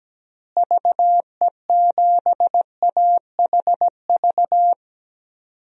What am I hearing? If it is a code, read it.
Morse code, heard as a keyed tone, about 17 wpm, 700 Hz. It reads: VE7AHV
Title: DXCC-100 Call Signs - 6 Characters - Repeated 2x - 17wpm